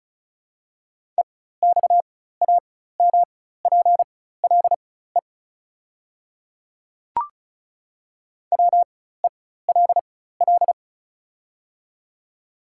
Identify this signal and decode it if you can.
Morse code, heard as a keyed tone, about 36 wpm, spaced as for 16 wpm, 700 Hz. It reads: EXAMPLE WELL
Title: 4x Character Spacing - Top 300 Words - 35wpm